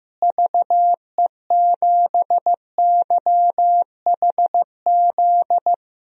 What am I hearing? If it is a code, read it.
Morse code, heard as a keyed tone, about 15 wpm, 700 Hz. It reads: VE7YHZ
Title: DXCC-100 Call Signs - 6 Characters - Repeated 3x - 15wpm